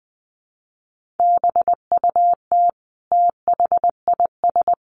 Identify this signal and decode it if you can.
Morse code, heard as a keyed tone, about 20 wpm, 700 Hz. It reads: BUT THIS
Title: Sentences from Top 1500 Words 20wpm